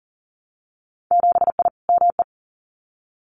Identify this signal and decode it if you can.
Morse code, heard as a keyed tone, about 40 wpm, 700 Hz. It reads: 7I ME
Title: Winter Field Day Exchanges Encore 40wpm